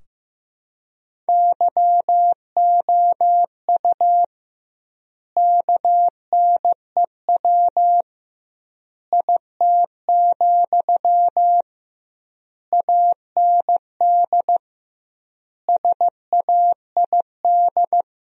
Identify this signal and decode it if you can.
Morse code, heard as a keyed tone, about 15 wpm, 700 Hz. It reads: YOU KNEW IT, AND SAID
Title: 2x Word-Spacing - Sentences from Top 500 Words - 15wpm